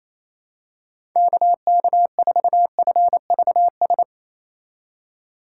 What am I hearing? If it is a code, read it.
Morse code, heard as a keyed tone, about 28 wpm, 700 Hz. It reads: KK4FVS
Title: USA Call Signs - 3x Character Spacing - 28wpm